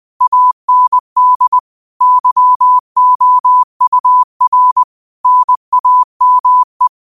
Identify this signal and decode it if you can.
Morse code, heard as a keyed tone, about 20 wpm, 1000 Hz. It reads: AND YOUR NAME
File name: qotdfiles / 20wpm